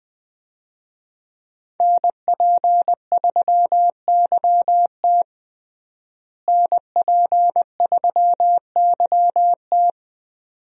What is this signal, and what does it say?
Morse code, heard as a keyed tone, about 21 wpm, 700 Hz. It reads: NP3YT NP3YT
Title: USA Call Signs - 5 Characters - Repeated 2x - 20wpm